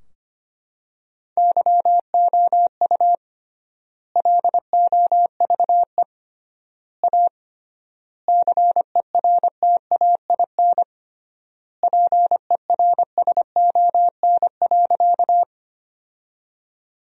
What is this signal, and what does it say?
Morse code, heard as a keyed tone, about 25 wpm, 700 Hz. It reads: YOU LOVE A CERTAIN PERSON.
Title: 3x Word-Spacing - Sentences from Top 500 Words - 25wpm